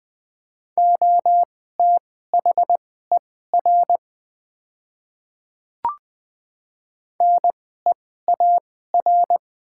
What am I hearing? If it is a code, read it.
Morse code, heard as a keyed tone, about 20 wpm, spaced as for 14 wpm, 700 Hz. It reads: OTHER NEAR
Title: 2x Character Spacing - Top 200 Words - 20wpm